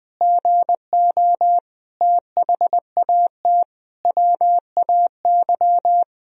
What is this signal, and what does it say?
Morse code, heard as a keyed tone, about 20 wpm, 700 Hz. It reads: GO THAT WAY
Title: Sentences from Top 1000 Words - Review Part 3 - 20wpm